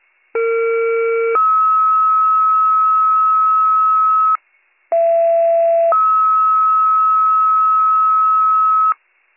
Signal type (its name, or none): two-tone page